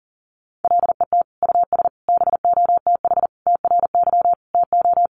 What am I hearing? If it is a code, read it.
Morse code, heard as a keyed tone, about 40 wpm, 700 Hz. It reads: LET US BOTH TRY TO
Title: Sentences from Top 1200 Words 40wpm